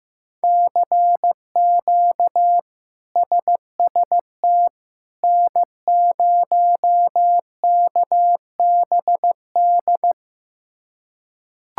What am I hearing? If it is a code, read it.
Morse code, heard as a keyed tone, about 15 wpm, 700 Hz. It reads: CQ SST N0KBD
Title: K1USN SST QSO Practice - 15wpm